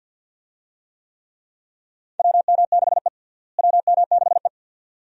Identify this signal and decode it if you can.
Morse code, heard as a keyed tone, about 47 wpm, 700 Hz. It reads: WM6E WM6E
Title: USA Call Signs - 4 Characters - Repeated 2x - 50wpm